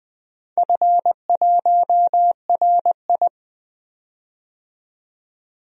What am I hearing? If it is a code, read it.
Morse code, heard as a keyed tone, about 20 wpm, 700 Hz. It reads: F1RI
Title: DXCC-100 Call Signs - 4 Characters - 20wpm